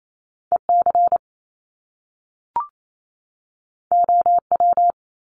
Morse code, heard as a keyed tone, about 28 wpm, 700 Hz. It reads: EC OW
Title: Top Two Letter Combinations in Words 28wpm